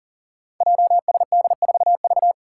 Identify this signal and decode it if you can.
Morse code, heard as a keyed tone, about 40 wpm, 700 Hz. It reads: JSD4V
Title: Five Challenging Letter-Number Combinations 40wpm